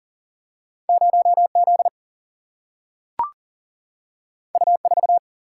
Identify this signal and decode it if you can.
Morse code, heard as a keyed tone, about 40 wpm, 700 Hz. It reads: 0 Z U 4
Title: Single-Letter Single-Number 40wpm